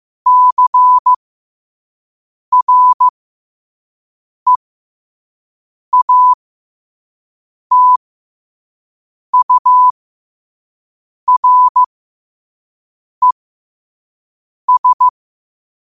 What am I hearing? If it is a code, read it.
Morse code, heard as a keyed tone, about 15 wpm, spaced as for 5 wpm, 1000 Hz. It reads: CREATURES
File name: qotdfiles / 5wpm